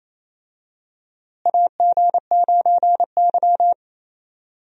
Morse code, heard as a keyed tone, about 28 wpm, 700 Hz. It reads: AG9Y
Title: USA Call Signs - 4 Characters - Repeated 2x - 28wpm